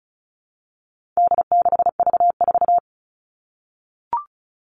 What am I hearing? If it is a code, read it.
Morse code, heard as a keyed tone, about 34 wpm, 700 Hz. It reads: D6V4
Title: All Cut Numbers - 4-Digit 35wpm